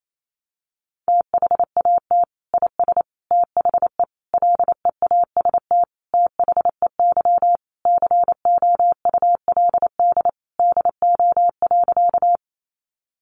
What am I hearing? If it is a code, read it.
Morse code, heard as a keyed tone, about 28 wpm, 700 Hz. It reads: THAT IS THE LEAST THEY COULD DO.
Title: Sentences from Top 700 Words 28wpm